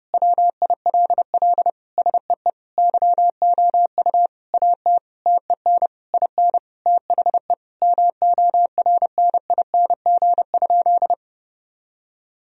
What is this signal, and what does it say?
Morse code, heard as a keyed tone, about 30 wpm, 700 Hz. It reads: WILL SEE YOU AT TEN IN THE MORNING?
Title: Sentences from Top 1000 Words - Review Part 2 - 30wpm